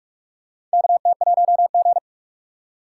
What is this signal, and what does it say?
Morse code, heard as a keyed tone, about 46 wpm, 700 Hz. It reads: KT1G
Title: USA Call Signs - 4 Characters - 3x Character Spacing - 45wpm